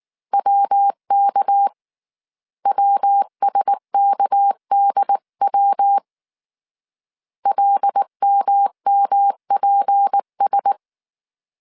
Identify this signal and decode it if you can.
Morse code, heard as a keyed tone, about 19 wpm, 800 Hz. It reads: WK WSKDW LMMPS
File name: CW / DET / 1 6 1500C S C 18wpm 2356